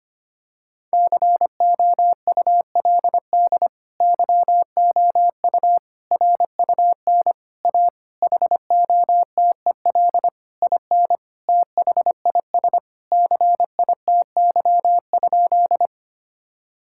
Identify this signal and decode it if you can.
Morse code, heard as a keyed tone, about 25 wpm, 700 Hz. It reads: COULD YOU RUN A HOTEL IN THIS CITY?
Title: Sentences from Top 1600 Words 25wpm